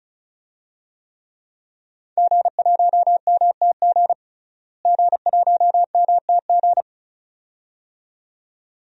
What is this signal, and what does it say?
Morse code, heard as a keyed tone, about 36 wpm, 700 Hz. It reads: G1MTG G1MTG
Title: DXCC-100 Call Signs - 5 Characters - Repeated 2x - 35wpm